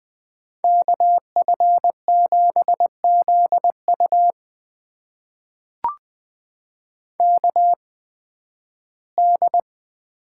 Morse code, heard as a keyed tone, about 20 wpm, spaced as for 13 wpm, 700 Hz. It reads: KF7ZU KD
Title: USA Call Signs - 5 Characters - 8x Character Spacing - 20wpm